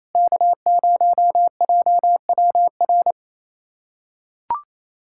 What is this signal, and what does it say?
Morse code, heard as a keyed tone, about 28 wpm, 700 Hz. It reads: K0JWR
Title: USA Callsign DE Callsign - 5 Characters - 28wpm